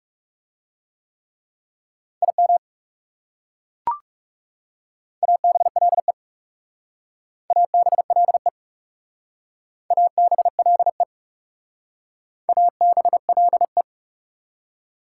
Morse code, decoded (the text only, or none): IM ABLE ABLE ABLE ABLE